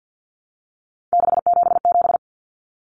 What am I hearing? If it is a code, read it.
Morse code, heard as a keyed tone, about 50 wpm, 700 Hz. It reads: 677
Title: Numbers - 3-Digit 50wpm